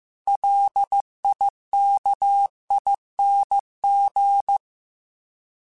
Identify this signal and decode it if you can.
Morse code, heard as a keyed tone, about 15 wpm, 800 Hz. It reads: LIKING
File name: Alphabetical / Alphabetical mp3s / Ls @13